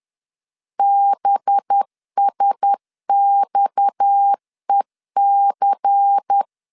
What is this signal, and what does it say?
Morse code, heard as a keyed tone, about 10 wpm, 800 Hz. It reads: BSXEC